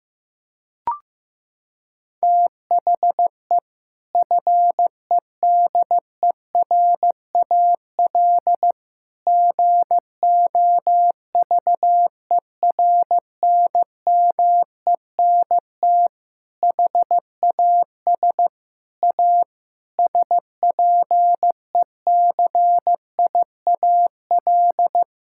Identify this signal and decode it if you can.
Morse code, heard as a keyed tone, about 15 wpm, 700 Hz. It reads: THE FEDERAL GOVERNMENT HAS A SPECIAL
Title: Sentences from Top 1000 Words - Review Part 6 - 15wpm